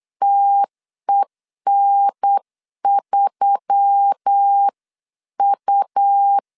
Morse code, heard as a keyed tone, about 8 wpm, 800 Hz. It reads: TEN3U
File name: CW / DET / 1 6 1500C S B 8wpm 5142